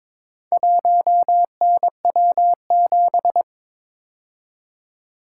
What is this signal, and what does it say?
Morse code, heard as a keyed tone, about 22 wpm, 700 Hz. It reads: 1NW7
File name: practice / 22wpm / Four Letter-Number Combinations - 22wpm